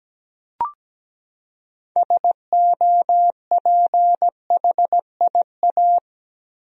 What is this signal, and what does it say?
Morse code, heard as a keyed tone, about 17 wpm, 700 Hz. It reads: SOPHIA
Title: USA Common Names - 17wpm